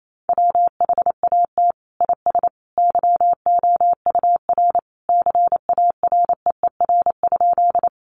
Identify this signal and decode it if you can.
Morse code, heard as a keyed tone, about 28 wpm, 700 Hz. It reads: WHAT IS YOUR CAREER?
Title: Sentences from Top 1000 Words 28wpm